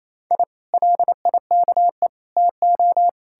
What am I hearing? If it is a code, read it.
Morse code, heard as a keyed tone, about 37 wpm, 700 Hz. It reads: I LIKE TO